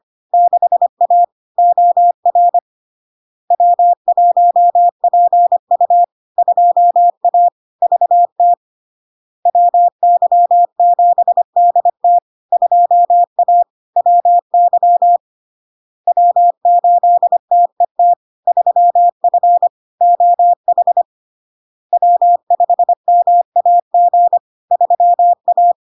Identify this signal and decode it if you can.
Morse code, heard as a keyed tone, about 25 wpm, 700 Hz. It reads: 6A OR W1PU 2A VT WY7DT 2A WY W8TET 3F OH W5MAG 3A
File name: Log-in-CW-25wpm-700-Hz